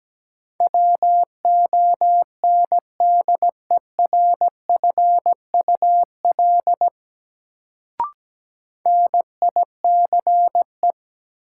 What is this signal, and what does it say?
Morse code, heard as a keyed tone, about 17 wpm, 700 Hz. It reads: WONDERFUL NICE
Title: Sentences from Top 1100 Words - New Word Drill Alt - 17wpm